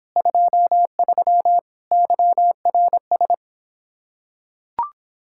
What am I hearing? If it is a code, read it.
Morse code, heard as a keyed tone, about 26 wpm, 700 Hz. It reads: 23 YRS